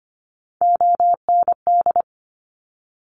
Morse code, heard as a keyed tone, about 25 wpm, 700 Hz. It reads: OND
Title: Top Three Letter Combinations in Words 25wpm